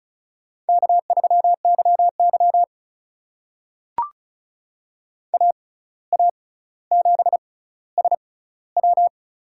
Morse code, heard as a keyed tone, about 35 wpm, spaced as for 17 wpm, 700 Hz. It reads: K3YY AA7SW
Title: USA Call Signs - 6x Character Spacing - 35wpm